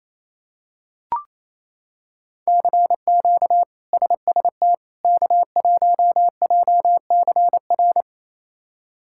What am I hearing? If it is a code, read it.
Morse code, heard as a keyed tone, about 28 wpm, 700 Hz. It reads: CQ SST K1JCR
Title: K1USN SST QSO Practice - 28wpm